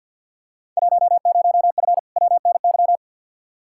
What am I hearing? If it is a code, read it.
Morse code, heard as a keyed tone, about 47 wpm, 700 Hz. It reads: 10F WNY
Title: Field Day Exchanges Encore - 50wpm